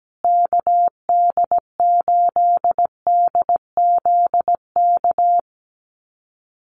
Morse code, heard as a keyed tone, about 17 wpm, 700 Hz. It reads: KD8DZK
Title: USA Call Signs Repeated 2x 17wpm